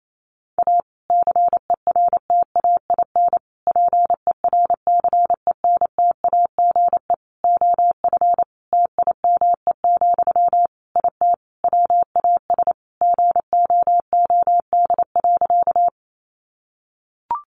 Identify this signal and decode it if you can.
Morse code, heard as a keyed tone, about 28 wpm, 700 Hz. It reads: A CERTAIN PERCENTAGE OF TIME, IT WAS GOOD.
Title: Sentences from Top 1300 Words 28wpm